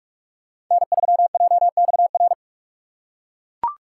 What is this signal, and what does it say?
Morse code, heard as a keyed tone, about 41 wpm, 700 Hz. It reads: N3JXR
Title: USA Call Signs - 5 Characters - 2x Character Spacing - 45wpm